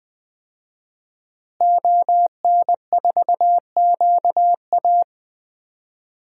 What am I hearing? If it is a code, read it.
Morse code, heard as a keyed tone, about 21 wpm, 700 Hz. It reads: ON4QA
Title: DXCC-100 Call Signs - 5 Characters - Repeated 2x - 20wpm